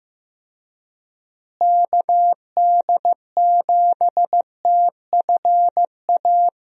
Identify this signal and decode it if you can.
Morse code, heard as a keyed tone, about 15 wpm, 700 Hz. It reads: KD7TFA
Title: USA Call Signs - 6 Characters - Repeated 3x - 15wpm